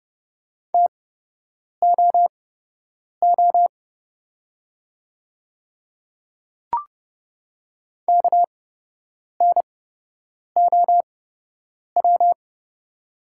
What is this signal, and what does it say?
Morse code, heard as a keyed tone, about 31 wpm, spaced as for 8 wpm, 700 Hz. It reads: TOO KNOW
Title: 8x Character Spacing - Top 200 Words - 30wpm